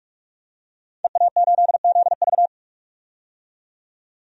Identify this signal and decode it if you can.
Morse code, heard as a keyed tone, about 46 wpm, 700 Hz. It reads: EA8ZV
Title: DXCC-100 Call Signs - 5 Characters - 45wpm